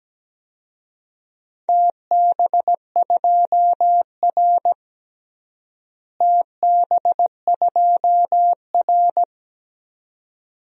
Morse code, heard as a keyed tone, about 17 wpm, 700 Hz. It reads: TB2R TB2R
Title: DXCC-100 Call Signs - 4 Characters - Repeated 2x - 17wpm